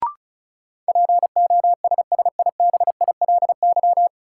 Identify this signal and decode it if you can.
Morse code, heard as a keyed tone, about 34 wpm, 700 Hz. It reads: POSSIBILY
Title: Sentences from Top 1300 Words - New Word Drill Alt - 35wpm